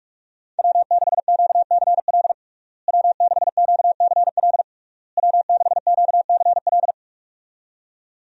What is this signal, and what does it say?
Morse code, heard as a keyed tone, about 46 wpm, 700 Hz. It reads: W6QCL W6QCL W6QCL
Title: USA Call Signs - 5 Characters - Repeated 3x - 45wpm